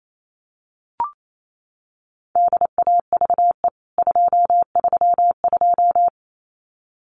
Morse code, heard as a keyed tone, about 28 wpm, 700 Hz. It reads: DAVE 232
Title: CWT Exchange 28wpm